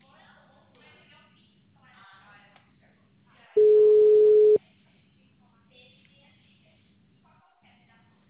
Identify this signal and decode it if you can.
Ringback tone (Europe, 425 Hz), with speech.